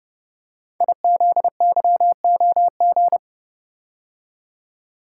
Morse code, heard as a keyed tone, about 30 wpm, 700 Hz. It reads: IZYOG
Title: Code Groups - 5 Letters - 30wpm